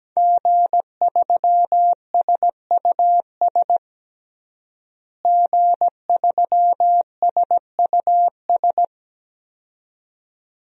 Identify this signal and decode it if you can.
Morse code, heard as a keyed tone, about 17 wpm, 700 Hz. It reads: G3SUS G3SUS
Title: DXCC-100 Call Signs - 5 Characters - Repeated 2x - 17wpm